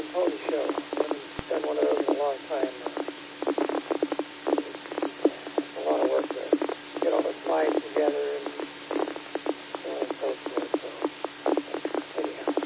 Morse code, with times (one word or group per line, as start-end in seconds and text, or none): none